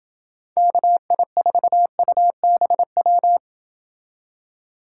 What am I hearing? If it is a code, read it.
Morse code, heard as a keyed tone, about 28 wpm, 700 Hz. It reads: KI4UBW